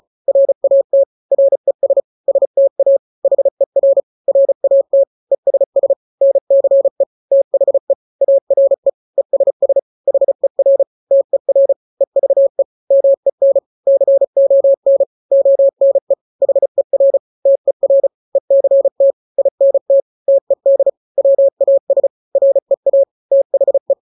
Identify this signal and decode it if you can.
Morse code, heard as a keyed tone, about 34 wpm, 550 Hz. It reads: RAT RES STA HER RAT ESS NCE THE ARE ESS HER TER EVE MEN CON ONE HER TER ECT INT TED WAS REA THE
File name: 3-grams 35 30 WPM